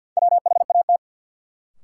Morse code, heard as a keyed tone, about 54 wpm, 700 Hz. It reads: WHAT
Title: Sentences from Top 100 Words - New Word Drill - 50wpm